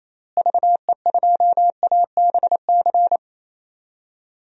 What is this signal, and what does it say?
Morse code, heard as a keyed tone, about 27 wpm, 700 Hz. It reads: VE2ABC
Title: DXCC-100 Call Signs - 6 Characters - Repeated 3x - 28wpm